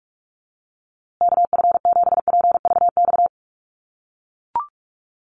Morse code, heard as a keyed tone, about 44 wpm, 700 Hz. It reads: KF7PVX
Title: USA Call Signs Repeated 3x 45wpm